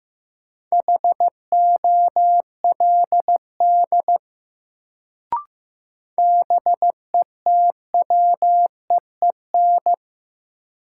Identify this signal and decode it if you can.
Morse code, heard as a keyed tone, about 15 wpm, 700 Hz. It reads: HOLD BETWEEN
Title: Sentences from Top 400 Words - New Word Drill - 15wpm